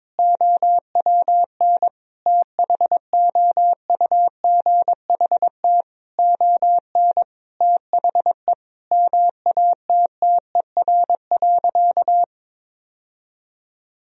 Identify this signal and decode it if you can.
Morse code, heard as a keyed tone, about 22 wpm, 700 Hz. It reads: OWN THOUGHT ON THE MATTER.
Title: Sentences from Top 1000 Words - Review Part 1 - 22wpm